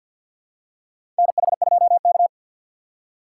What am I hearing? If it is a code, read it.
Morse code, heard as a keyed tone, about 48 wpm, 700 Hz. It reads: NH2K
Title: USA Call Signs - 4 Characters - 3x Character Spacing - 50wpm